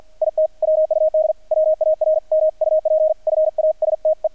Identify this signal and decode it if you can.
Morse code, heard as a keyed tone, about 50 wpm, 630 Hz. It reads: IT WUD WARMUP FASTE
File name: archived / CRSnet